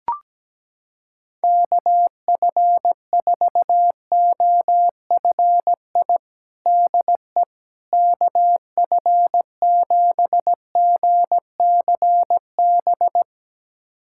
Morse code, heard as a keyed tone, about 17 wpm, 700 Hz. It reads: KF4OFI DE KF7GCB
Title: USA Callsign DE Callsign - 6 Characters - 17wpm